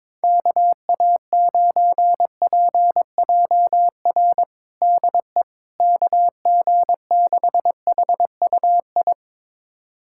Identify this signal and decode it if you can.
Morse code, heard as a keyed tone, about 21 wpm, 700 Hz. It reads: KA9PJR DE KG6HUI
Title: USA Callsign DE Callsign 22wpm